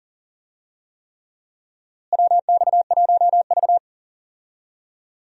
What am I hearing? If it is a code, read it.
Morse code, heard as a keyed tone, about 40 wpm, 700 Hz. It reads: WX1V
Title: USA Call Signs - 4 Characters - Repeated 3x - 40wpm